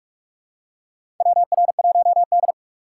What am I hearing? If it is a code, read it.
Morse code, heard as a keyed tone, about 45 wpm, 700 Hz. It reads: WR1D
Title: USA Call Signs - 4 Characters - 2x Character Spacing - 45wpm